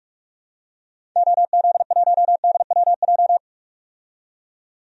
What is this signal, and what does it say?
Morse code, heard as a keyed tone, about 41 wpm, 700 Hz. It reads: OZ1DWJ
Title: DXCC-100 Call Signs - 6 Characters - 45wpm